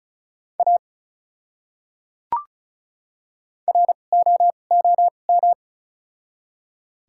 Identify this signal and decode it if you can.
Morse code, heard as a keyed tone, about 36 wpm, spaced as for 25 wpm, 700 Hz. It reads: A ROOM
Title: 2x Character Spacing - Top 500 Words - 35wpm